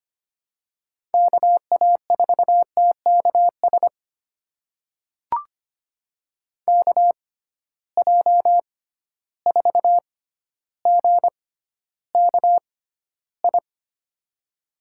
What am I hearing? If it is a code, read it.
Morse code, heard as a keyed tone, about 25 wpm, spaced as for 13 wpm, 700 Hz. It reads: KA4TKS KJ4GKI
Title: USA Call Signs - 6 Characters - 6x Character Spacing - 25wpm